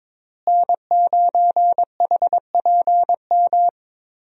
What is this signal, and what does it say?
Morse code, heard as a keyed tone, about 22 wpm, 700 Hz. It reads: N9HPM
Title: USA Call Signs - 5 Characters - Repeated 3x - 22wpm